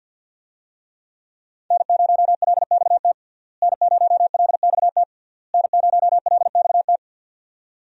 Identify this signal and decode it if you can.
Morse code, heard as a keyed tone, about 50 wpm, 700 Hz. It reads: N0LXT N0LXT N0LXT
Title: USA Call Signs - 5 Characters - Repeated 3x - 50wpm